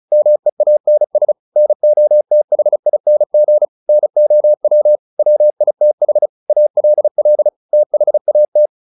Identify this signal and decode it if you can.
Morse code, heard as a keyed tone, about 34 wpm, 600 Hz. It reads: MEANS NOTHING NOW WITH ALL THAT
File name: the old man and the sea 2 35wpm